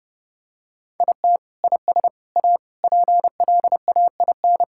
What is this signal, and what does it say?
Morse code, heard as a keyed tone, about 32 wpm, 700 Hz. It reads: IT IS A PLAIN
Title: Sentences from Top 1000 Words - Review Part 5 - 30wpm